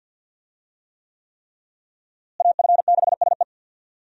Morse code, heard as a keyed tone, about 53 wpm, 700 Hz. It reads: AF6SE
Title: USA Call Signs - 5 Characters - Repeated 3x - 50wpm